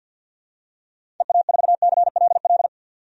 Morse code, heard as a keyed tone, about 52 wpm, 700 Hz. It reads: EA4CLL
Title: DXCC-100 Call Signs - 6 Characters - 50wpm